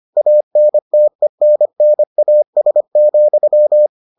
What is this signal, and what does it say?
Morse code, heard as a keyed tone, about 25 wpm, 600 Hz. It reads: ANTENNAS,